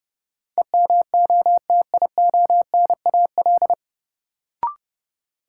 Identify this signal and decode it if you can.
Morse code, heard as a keyed tone, about 32 wpm, 700 Hz. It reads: EMOTIONAL
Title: Sentences from Top 1000 Words - Review Part 5 - New Word Drill - 30wpm